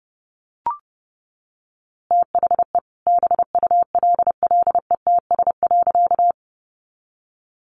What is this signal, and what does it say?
Morse code, heard as a keyed tone, about 30 wpm, 700 Hz. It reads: THE BULLETS.
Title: A Sound Of Thunder 30wpm